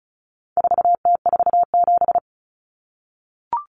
Morse code, heard as a keyed tone, about 34 wpm, 700 Hz. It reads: 4T47
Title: Common Cut Numbers - 4-Digit 35wpm